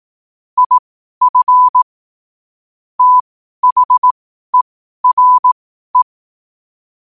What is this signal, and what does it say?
Morse code, heard as a keyed tone, about 18 wpm, spaced as for 12 wpm, 1000 Hz. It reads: IF THERE
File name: qotdfiles / 12wpm